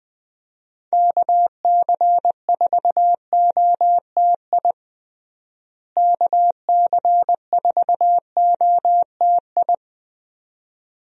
Morse code, heard as a keyed tone, about 21 wpm, 700 Hz. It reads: KC4OTI KC4OTI
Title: USA Call Signs - 6 Characters - Repeated 2x - 20wpm